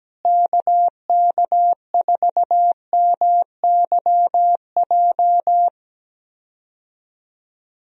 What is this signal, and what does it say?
Morse code, heard as a keyed tone, about 17 wpm, 700 Hz. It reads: KK4MYJ
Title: USA Call Signs - 6 Characters - Repeated 2x - 17wpm